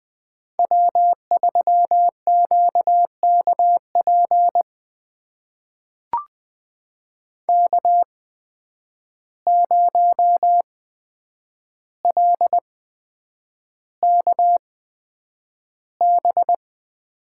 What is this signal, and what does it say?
Morse code, heard as a keyed tone, about 20 wpm, spaced as for 9 wpm, 700 Hz. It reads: W3QKP K0LKB
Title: USA Call Signs - 5 Characters - 8x Character Spacing - 20wpm